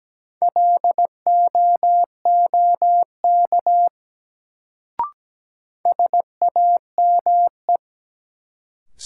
Morse code, heard as a keyed tone, about 17 wpm, 700 Hz. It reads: LOOK SAME
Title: Sets of 2 Words - New Word Drill - 17wpm